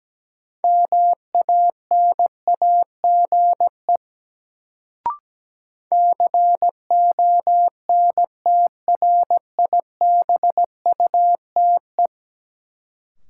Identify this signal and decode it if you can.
Morse code, heard as a keyed tone, about 17 wpm, 700 Hz. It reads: MANAGE CONTRIBUTE
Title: Sentences from Top 1400 Words - New Word Drill - 17wpm